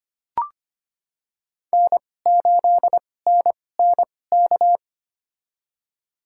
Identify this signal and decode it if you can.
Morse code, heard as a keyed tone, about 25 wpm, spaced as for 18 wpm, 700 Hz. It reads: N8NNK
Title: USA Call Signs - 5 Characters - 2x Character Spacing - 25wpm